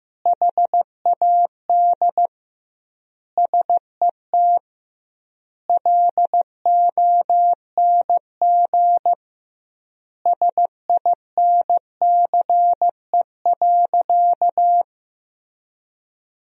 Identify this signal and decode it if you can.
Morse code, heard as a keyed tone, about 15 wpm, 700 Hz. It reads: HAD SET LONG SINCE.
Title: 2x Word-Spacing - Sentences from Top 300 Words - 15wpm